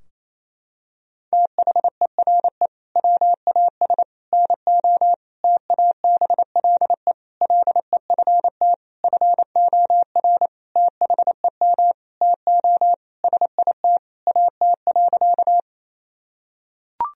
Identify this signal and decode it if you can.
Morse code, heard as a keyed tone, about 28 wpm, 700 Hz. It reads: THERE WAS NO TABLE LEFT FOR THEM TO SIT AT.
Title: Sentences from Top 1000 Words - Review Part 2 - 28wpm